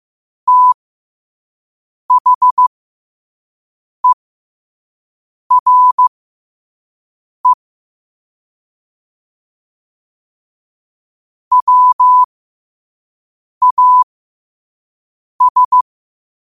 Morse code, heard as a keyed tone, about 15 wpm, spaced as for 5 wpm, 1000 Hz. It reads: THERE WAS
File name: qotdfiles / 5wpm